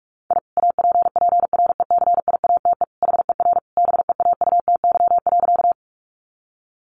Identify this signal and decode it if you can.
Morse code, heard as a keyed tone, about 44 wpm, 700 Hz. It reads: I APPRECIATE HER BEAUTY.
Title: Sentences from Top 1300 Words 45wpm